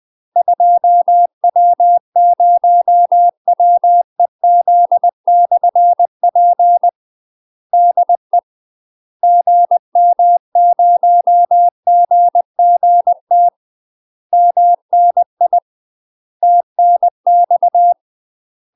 Morse code, heard as a keyed tone, about 20 wpm, 700 Hz. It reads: 2W0WEZ/P DE GM0GGT MNI TNX
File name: UK MASTER QSOs TRAINING / 20WPM / UK MASTER QSOs 0048